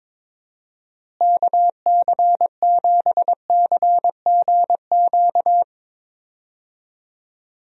Morse code, heard as a keyed tone, about 21 wpm, 700 Hz. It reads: KC7CGQ
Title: USA Call Signs - 6 Characters - Repeated 2x - 22wpm